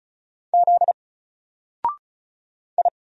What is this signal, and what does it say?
Morse code, heard as a keyed tone, about 35 wpm, 700 Hz. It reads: Z I
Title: Single Letter-Number Alt 35wpm